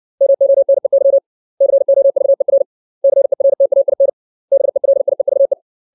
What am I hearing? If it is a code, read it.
Morse code, heard as a keyed tone, about 60 wpm, 550 Hz. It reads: MONEY COVER CERTAIN BELIEVE
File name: 632 words / 632 Words 60WPM